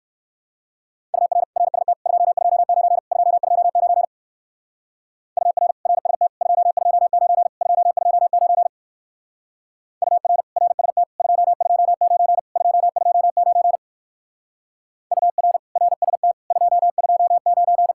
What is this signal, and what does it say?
Morse code, heard as a keyed tone, about 59 wpm, 700 Hz. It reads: UR RST 229 229 UR RST 229 229 UR RST 229 229 UR RST 229